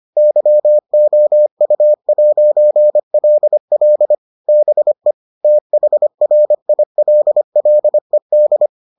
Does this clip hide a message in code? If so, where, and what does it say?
Yes: Morse code from 0.2 to 8.7 s, heard as a keyed tone, about 25 wpm, 600 Hz. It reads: YOU'LL BE THRILLED